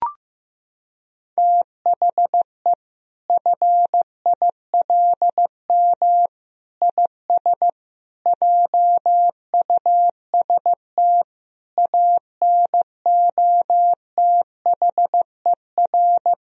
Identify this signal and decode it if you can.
Morse code, heard as a keyed tone, about 15 wpm, 700 Hz. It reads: THE FILM IS JUST ANOTHER
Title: Sentences from Top 1000 Words - Review Part 3 - 15wpm